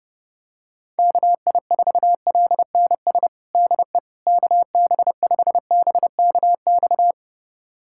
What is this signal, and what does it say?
Morse code, heard as a keyed tone, about 32 wpm, 700 Hz. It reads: KI4LNS DE KB5BKX
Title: USA Callsign DE Callsign 30wpm